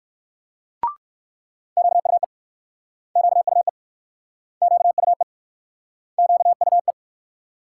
Morse code, heard as a keyed tone, about 61 wpm, 700 Hz. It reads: QUE QUE QUE QUE